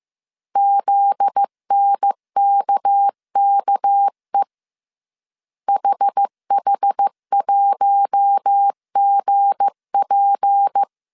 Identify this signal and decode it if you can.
Morse code, heard as a keyed tone, about 15 wpm, 800 Hz. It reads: ZNKKE HH1GP
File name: CW / DET / 1 9 1500C S A 14wpm 3222